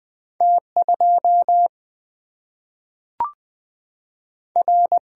Morse code, heard as a keyed tone, about 20 wpm, 700 Hz. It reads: T 2 R
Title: Single-Letter Single-Number 20wpm